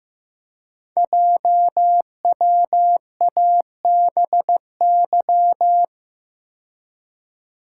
Morse code, heard as a keyed tone, about 15 wpm, 700 Hz. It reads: JWABY
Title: Code Groups - 5 Letters - 15wpm